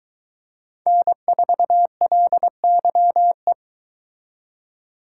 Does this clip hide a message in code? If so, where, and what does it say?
Yes: Morse code from 0.9 to 3.5 s, heard as a keyed tone, about 22 wpm, 700 Hz. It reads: N4LYE